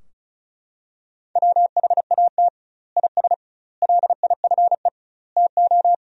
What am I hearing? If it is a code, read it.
Morse code, heard as a keyed tone, about 36 wpm, 700 Hz. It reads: WHAT IS LIFE TO
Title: 2x Word-Spacing - Sentences from Top 300 Words - 35wpm